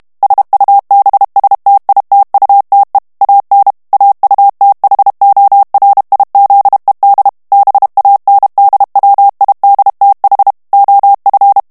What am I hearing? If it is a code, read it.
Morse code, heard as a keyed tone, about 32 wpm, 800 Hz. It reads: SUBSTITUTE AN AUTHORIZED BANDWIDTH OF
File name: ANGLAIS texte clair 3 vitesse 30 mots minute